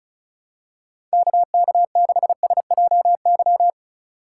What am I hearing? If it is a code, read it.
Morse code, heard as a keyed tone, about 34 wpm, 700 Hz. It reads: KK6SJY
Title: USA Call Signs - 6 Characters - 3x Character Spacing - 35wpm